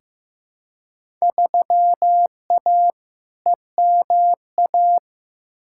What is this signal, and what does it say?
Morse code, heard as a keyed tone, about 15 wpm, 700 Hz. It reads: 3A EMA
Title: Field Day Exchanges - 15wpm